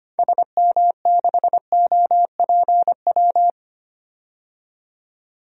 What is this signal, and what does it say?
Morse code, heard as a keyed tone, about 25 wpm, 700 Hz. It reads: SM6OPW
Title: DXCC-100 Call Signs - 6 Characters - 25wpm